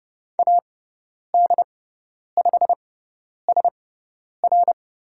Morse code, heard as a keyed tone, about 30 wpm, spaced as for 10 wpm, 700 Hz. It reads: AD5SR
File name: Lesson 17g - Letter u - Call Signs 30wpm